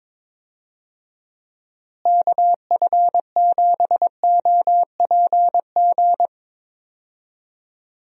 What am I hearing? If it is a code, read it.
Morse code, heard as a keyed tone, about 22 wpm, 700 Hz. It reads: KF7OPG
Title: USA Call Signs - 6 Characters - Repeated 3x - 22wpm